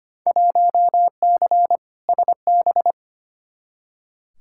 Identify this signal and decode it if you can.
Morse code, heard as a keyed tone, about 25 wpm, 700 Hz. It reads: 1C SB
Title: Field Day Exchanges Encore - 25wpm